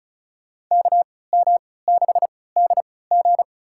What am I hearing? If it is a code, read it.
Morse code, heard as a keyed tone, about 34 wpm, spaced as for 20 wpm, 700 Hz. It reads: KM6DG
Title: USA Call Signs - 5 Characters - 3x Character Spacing - 35wpm